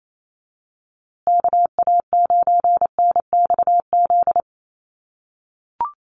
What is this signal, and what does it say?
Morse code, heard as a keyed tone, about 28 wpm, 700 Hz. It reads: KA9NXZ
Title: USA Call Signs Repeated 2x 28wpm